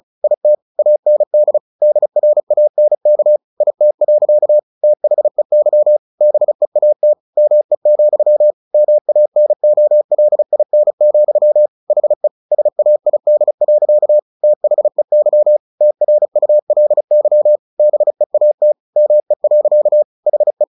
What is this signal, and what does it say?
Morse code, heard as a keyed tone, about 34 wpm, 600 Hz. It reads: IT AND DRANK IT. THEY BEAT ME, MANOLIN, HE SAID. THEY TRULY BEAT ME. HE
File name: the old man and the sea 2 35wpm